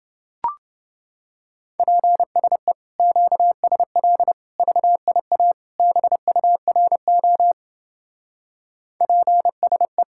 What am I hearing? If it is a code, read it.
Morse code, heard as a keyed tone, about 30 wpm, 700 Hz. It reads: PSE QSL VIA BURO PSE